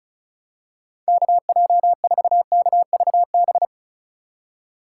Morse code, heard as a keyed tone, about 36 wpm, 700 Hz. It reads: KJ4KVB
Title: USA Call Signs - 2x Character Spacing - 35wpm